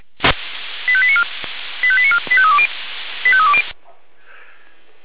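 Five-tone selective call (ZVEI), heard as quick runs of five tones, tones of about 70 ms. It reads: 85894 85894 85320 85320
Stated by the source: Five-tone ZVEI selective call received on an SDR, logged by fire brigade Feuerwehr Orlamünde.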